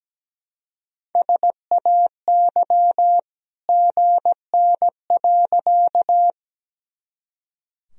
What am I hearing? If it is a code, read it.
Morse code, heard as a keyed tone, about 17 wpm, 700 Hz. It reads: SAY GN.
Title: Sentences from Top 300 Words and Abbreviations 17wpm